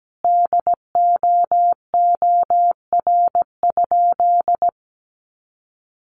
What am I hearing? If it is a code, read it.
Morse code, heard as a keyed tone, about 17 wpm, 700 Hz. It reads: DOOR?
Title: Sentences from Top 900 Words 17wpm